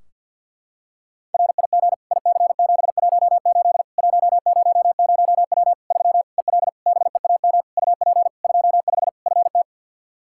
Morse code, heard as a keyed tone, about 53 wpm, 700 Hz. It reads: RIG IC718 100W 3 EL BEAM UP 25 FT
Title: Practice QSO - Exchange 2 - Info - 50wpm